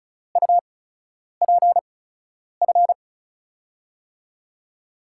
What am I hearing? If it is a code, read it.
Morse code, heard as a keyed tone, about 36 wpm, spaced as for 10 wpm, 700 Hz. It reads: UPF
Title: USA Call Signs - 6 Characters - 8x Character Spacing - 35wpm